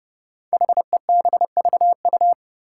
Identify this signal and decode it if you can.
Morse code, heard as a keyed tone, about 31 wpm, 700 Hz. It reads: HEBVU
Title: Five Challenging Letter-Number Combinations 30wpm